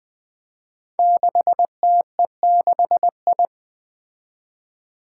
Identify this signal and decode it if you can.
Morse code, heard as a keyed tone, about 20 wpm, 700 Hz. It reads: 6TE6I
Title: Five Challenging Letter-Number Combinations 20wpm